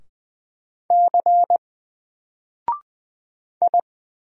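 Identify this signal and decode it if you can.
Morse code, heard as a keyed tone, about 20 wpm, 700 Hz. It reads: C I
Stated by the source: Single letter and number Morse drill at 20 wpm, from N4LCD.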